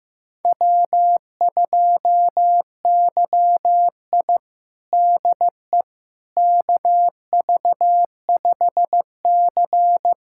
Morse code, heard as a keyed tone, about 15 wpm, 700 Hz. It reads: W2YI DE KV5C
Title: USA Callsign DE Callsign - 4 Characters - 15wpm